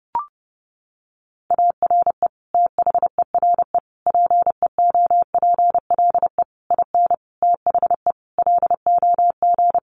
Morse code, heard as a keyed tone, about 30 wpm, 700 Hz. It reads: ARE THERE PEOPLE IN THE LOG
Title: Sentences from Top 1400 Words 30wpm